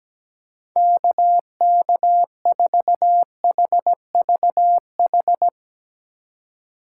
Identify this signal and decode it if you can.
Morse code, heard as a keyed tone, about 17 wpm, 700 Hz. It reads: KK4HVH
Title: USA Call Signs - 6x Character Spacing - 17wpm